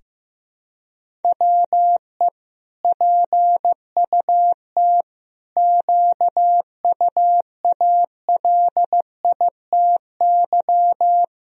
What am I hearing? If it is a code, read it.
Morse code, heard as a keyed tone, about 15 wpm, 700 Hz. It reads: WE PUT QUALITY
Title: Sentences from Top 1000 Words - Review Part 4 - 15wpm